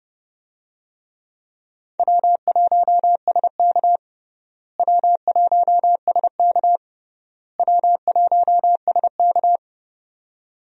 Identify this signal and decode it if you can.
Morse code, heard as a keyed tone, about 30 wpm, 700 Hz. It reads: W1SK W1SK W1SK
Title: USA Call Signs - 4 Characters - Repeated 3x - 30wpm